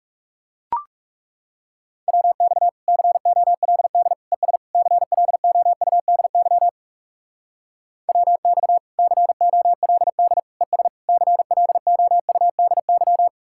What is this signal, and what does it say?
Morse code, heard as a keyed tone, about 38 wpm, 700 Hz. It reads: WX COLD ES CLOUDY WX COLD ES CLOUDY